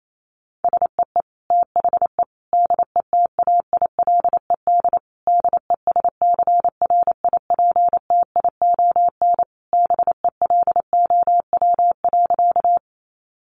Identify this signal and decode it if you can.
Morse code, heard as a keyed tone, about 28 wpm, 700 Hz. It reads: SEE THE DETAILED DESCRIPTION BELOW.
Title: Sentences from Top 1500 Words 28wpm